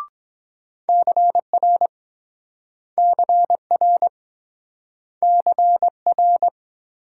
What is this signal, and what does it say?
Morse code, heard as a keyed tone, about 23 wpm, 700 Hz. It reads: CR CR CR